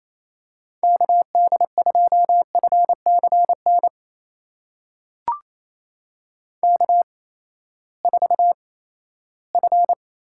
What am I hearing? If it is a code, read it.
Morse code, heard as a keyed tone, about 28 wpm, spaced as for 16 wpm, 700 Hz. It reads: KD2FCN K4F
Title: USA Call Signs - 8x Character Spacing - 28wpm